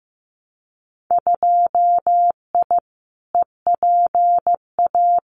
Morse code, heard as a keyed tone, about 15 wpm, 700 Hz. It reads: 2I EPA
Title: Winter Field Day Exchanges 15wpm